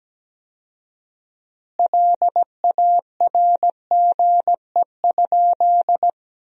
Morse code, heard as a keyed tone, about 17 wpm, 700 Hz. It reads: LARGE?
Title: 4x Word-Spacing - Sentences from Top 200 Words - 17wpm